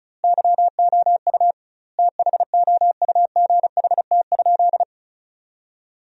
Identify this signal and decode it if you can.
Morse code, heard as a keyed tone, about 36 wpm, 700 Hz. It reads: YOU THOUGHT?
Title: 2x Word-Spacing - Sentences from Top 300 Words - 35wpm